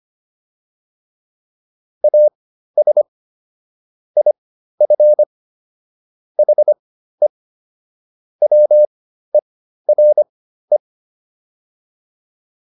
Morse code, heard as a keyed tone, about 25 wpm, spaced as for 13 wpm, 600 Hz. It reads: AS IF HE WERE
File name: WD302 13